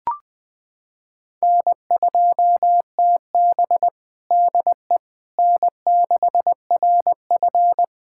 Morse code, heard as a keyed tone, about 20 wpm, 700 Hz. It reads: N2TB DE N6RF
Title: USA Callsign DE Callsign - 4 Characters - 20wpm